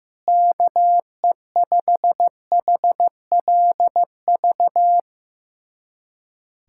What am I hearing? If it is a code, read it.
Morse code, heard as a keyed tone, about 15 wpm, 700 Hz. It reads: KE5HLV
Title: USA Callsign DE Callsign - 6 Characters - 15wpm